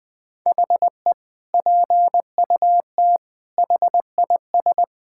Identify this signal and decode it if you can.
Morse code, heard as a keyed tone, about 20 wpm, 700 Hz. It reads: HE PUT HIS
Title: Sentences from Top 1000 Words - Review Part 6 - 20wpm